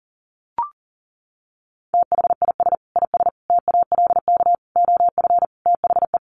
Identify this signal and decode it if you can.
Morse code, heard as a keyed tone, about 37 wpm, 700 Hz. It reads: THIS IS TALK OF THE
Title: Sets of 5 Words 40wpm